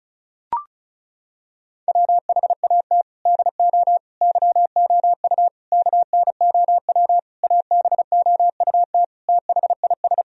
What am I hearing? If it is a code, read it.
Morse code, heard as a keyed tone, about 36 wpm, 700 Hz. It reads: WHAT DO YOU KNOW ABOUT THIS
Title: Sentences from Top 1000 Words - Review Part 2 - 35wpm